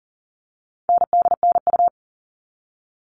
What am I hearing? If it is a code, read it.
Morse code, heard as a keyed tone, about 40 wpm, 700 Hz. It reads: NDNU
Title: All Cut Numbers - 4-Digit 40wpm